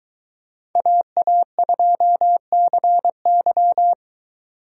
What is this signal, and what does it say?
Morse code, heard as a keyed tone, about 23 wpm, 700 Hz. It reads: AA2CY